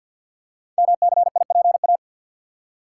Morse code, heard as a keyed tone, about 52 wpm, 700 Hz. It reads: MCIPA
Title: Code Groups - 5 Letters - 50wpm